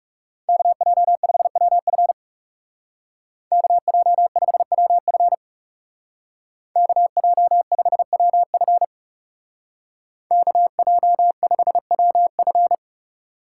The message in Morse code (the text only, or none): KJ5WF KJ5WF KJ5WF KJ5WF